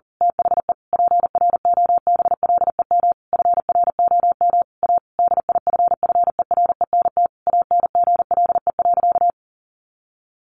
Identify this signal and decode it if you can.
Morse code, heard as a keyed tone, about 40 wpm, 700 Hz. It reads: THE PROBLEM FROM A DIFFERENT ANGLE.
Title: Sentences from Top 1500 Words 40wpm